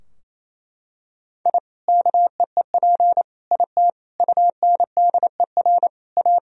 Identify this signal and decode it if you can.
Morse code, heard as a keyed tone, about 28 wpm, 700 Hz. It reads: I KEEP IT UNDER A
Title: Sentences from Top 1000 Words - Review Part 2 - 28wpm